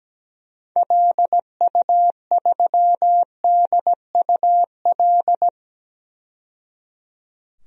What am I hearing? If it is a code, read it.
Morse code, heard as a keyed tone, about 17 wpm, 700 Hz. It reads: LU3DUL
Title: DXCC-100 Call Signs - 6 Characters - 17wpm